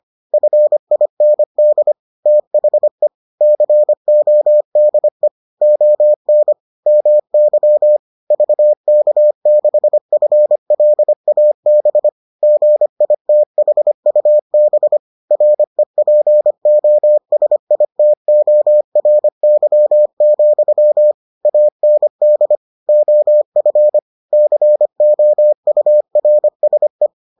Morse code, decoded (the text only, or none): FIND THE CODE ON MY VK6FLAB GITHUB REPOSITORY, AND OF COURSE